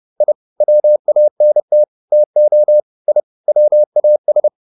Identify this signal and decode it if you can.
Morse code, heard as a keyed tone, about 30 wpm, 600 Hz. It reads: I WANT TO I WAS